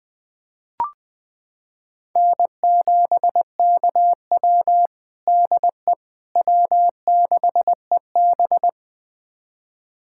Morse code, heard as a keyed tone, about 21 wpm, 700 Hz. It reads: N7KW DE W6EB
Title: USA Callsign DE Callsign - 4 Characters - 20wpm